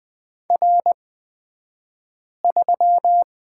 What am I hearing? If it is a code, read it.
Morse code, heard as a keyed tone, about 20 wpm, spaced as for 5 wpm, 700 Hz. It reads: R3
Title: 228 Lesson 37d - Letter x - Two Characters 20wpm